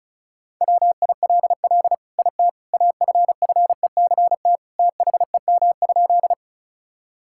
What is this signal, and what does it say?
Morse code, heard as a keyed tone, about 36 wpm, 700 Hz. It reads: WILL IT AFFECT THEM?
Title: Sentences from Top 1000 Words - Review Part 6 - 35wpm